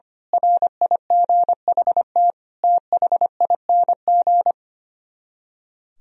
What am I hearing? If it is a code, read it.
Morse code, heard as a keyed tone, about 24 wpm, 700 Hz. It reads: RIGHT THING
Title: Speed-Racing - Sets of 5 Words 25wpm